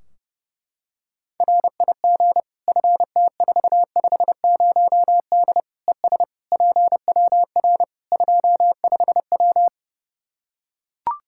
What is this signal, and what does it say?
Morse code, heard as a keyed tone, about 32 wpm, 700 Hz. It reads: RIG FT450D ES PWR 25W
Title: K1USN SST QSO Practice 30wpm